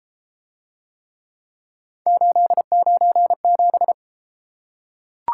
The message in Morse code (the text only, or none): 897